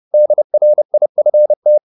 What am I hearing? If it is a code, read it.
Morse code, heard as a keyed tone, about 30 wpm, 600 Hz. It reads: DRIFT
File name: the old man and the sea / the old man and the sea 1 30wpm